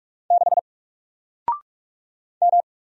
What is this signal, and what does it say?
Morse code, heard as a keyed tone, about 45 wpm, 700 Hz. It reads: 6 M
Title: Single Letter-Number 45wpm